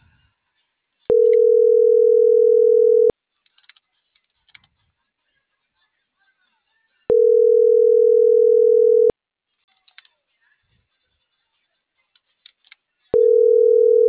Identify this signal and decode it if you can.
Ringback tone (North America), 440 + 480 Hz.